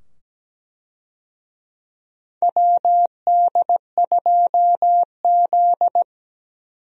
Morse code, heard as a keyed tone, about 17 wpm, 700 Hz. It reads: WD2Z
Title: USA Call Signs - 4 Characters - 17wpm